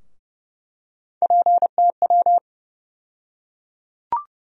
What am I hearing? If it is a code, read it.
Morse code, heard as a keyed tone, about 30 wpm, 700 Hz. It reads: PTW
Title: USA Call Sign Postfixes - 3 Letters 30wpm